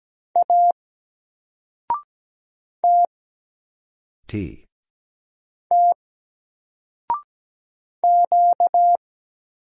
Morse code, heard as a keyed tone, about 17 wpm, 700 Hz. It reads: A T T Q
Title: Single Letter-Number 17wpm